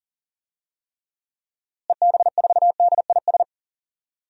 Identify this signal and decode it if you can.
Morse code, heard as a keyed tone, about 37 wpm, 700 Hz. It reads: EB4DIS
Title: DXCC-100 Call Signs - 6 Characters - Repeated 3x - 40wpm